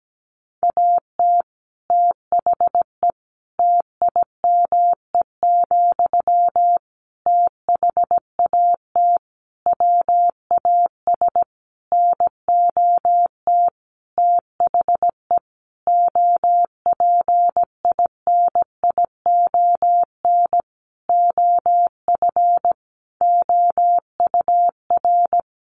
Morse code, heard as a keyed tone, about 17 wpm, 700 Hz. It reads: AT THE TIME, THAT WAS NOT THE OPINION OF OUR
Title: Sentences from Top 1400 Words 17wpm